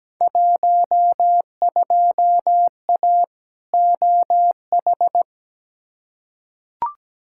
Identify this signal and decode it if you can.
Morse code, heard as a keyed tone, about 17 wpm, 700 Hz. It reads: 12A OH
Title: Field Day Exchanges Encore - 17wpm